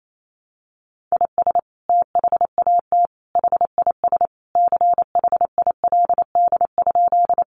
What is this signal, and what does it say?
Morse code, heard as a keyed tone, about 28 wpm, 700 Hz. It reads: IS THAT HIS CHILD?
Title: Sentences from Top 800 Words 28wpm